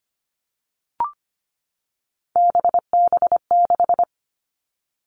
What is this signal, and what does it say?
Morse code, heard as a keyed tone, about 25 wpm, 700 Hz. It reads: BB6
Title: All Cut Numbers - 3-Digit 25wpm